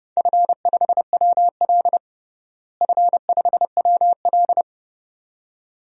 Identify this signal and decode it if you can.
Morse code, heard as a keyed tone, about 30 wpm, 700 Hz. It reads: F5WL F5WL
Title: DXCC-100 Call Signs - 4 Characters - Repeated 2x - 30wpm